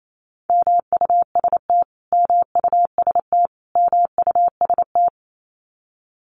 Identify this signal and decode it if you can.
Morse code, heard as a keyed tone, about 28 wpm, 700 Hz. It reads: MUST MUST MUST
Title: Top 500 Words Repeated 28wpm